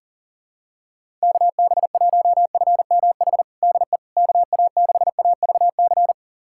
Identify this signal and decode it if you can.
Morse code, heard as a keyed tone, about 37 wpm, 700 Hz. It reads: KB1FMH DE KA6AVC
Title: USA Callsign DE Callsign - 6 Characters - 40wpm